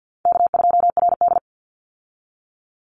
Morse code, heard as a keyed tone, about 48 wpm, 700 Hz. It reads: K2RD
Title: Call Signs Repeated 3x - 50wpm